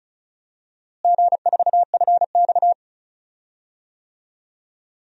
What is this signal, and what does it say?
Morse code, heard as a keyed tone, about 36 wpm, 700 Hz. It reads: G4FX
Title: DXCC-100 Call Signs - 4 Characters - Repeated 2x - 35wpm